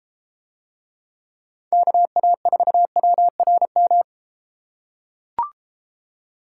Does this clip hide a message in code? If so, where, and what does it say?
Yes: Morse code from 1.7 to 4.0 s, heard as a keyed tone, about 32 wpm, 700 Hz. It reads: KA4WRM